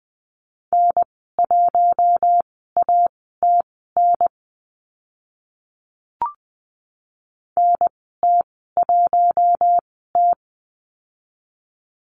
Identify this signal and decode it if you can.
Morse code, heard as a keyed tone, about 20 wpm, spaced as for 14 wpm, 700 Hz. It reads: N1ATN NT1T
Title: 042 Lesson 8g - Number 1 - Call Signs 20wpm